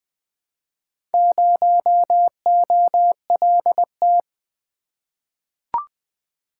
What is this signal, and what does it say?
Morse code, heard as a keyed tone, about 20 wpm, 700 Hz. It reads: 0OLT
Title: Four Letter-Number Combinations - 20wpm